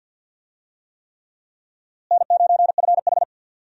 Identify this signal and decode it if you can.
Morse code, heard as a keyed tone, about 52 wpm, 700 Hz. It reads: N9FH
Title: USA Call Signs - 4 Characters - Repeated 3x - 50wpm